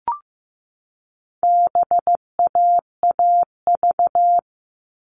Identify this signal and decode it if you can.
Morse code, heard as a keyed tone, about 15 wpm, 700 Hz. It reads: BAAV
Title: All Cut Numbers - 4-Digit 15wpm